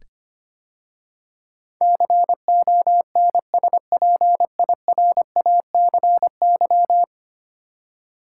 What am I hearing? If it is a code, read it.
Morse code, heard as a keyed tone, about 25 wpm, 700 Hz. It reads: CONSPIRACY
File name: 10 Letter Words - First 3 Letters Given - 25wpm